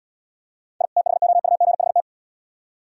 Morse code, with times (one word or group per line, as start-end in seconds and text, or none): 0.8-0.9 I
1.0-2.0 THOUGHT